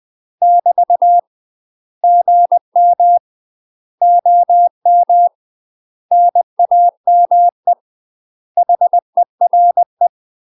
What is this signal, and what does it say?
Morse code, heard as a keyed tone, about 20 wpm, 700 Hz. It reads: <BT> GM OM NAME HERE
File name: UK MASTER QSOs TRAINING / 20WPM / UK MASTER QSOs 0032